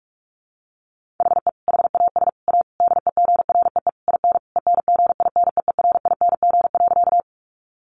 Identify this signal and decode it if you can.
Morse code, heard as a keyed tone, about 46 wpm, 700 Hz. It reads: HE HAS A DEGREE IN ENGINEERING.
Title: Sentences from Top 1500 Words 45wpm